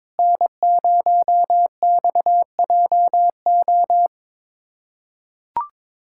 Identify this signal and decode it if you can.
Morse code, heard as a keyed tone, about 22 wpm, 700 Hz. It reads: N0XJO
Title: USA Callsign DE Callsign - 5 Characters - 22wpm